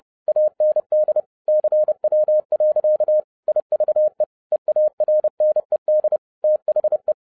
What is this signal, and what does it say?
Morse code, heard as a keyed tone, about 30 wpm, 610 Hz. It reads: AND CW. IVE EARNED THE
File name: 2012-01-mp3 / n11-I DIDNT MISS THE SOLAR CYCLE 24-30wpm